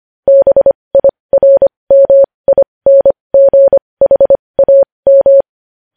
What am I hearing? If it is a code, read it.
Morse code, heard as a keyed tone, about 25 wpm, spaced as for 20 wpm, 550 Hz. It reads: BIRMINGHAM
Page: News headlines in morse code at 20 wpm